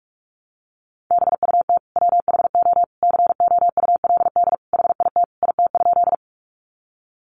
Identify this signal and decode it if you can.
Morse code, heard as a keyed tone, about 44 wpm, 700 Hz. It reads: BUT WHO COULD HIT IT?
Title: Sentences from Top 1200 Words 45wpm